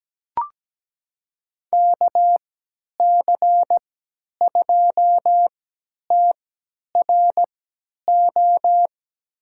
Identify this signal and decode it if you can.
Morse code, heard as a keyed tone, about 17 wpm, spaced as for 10 wpm, 700 Hz. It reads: KC2TRO
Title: USA Call Signs - 6 Characters - 3x Character Spacing - 17wpm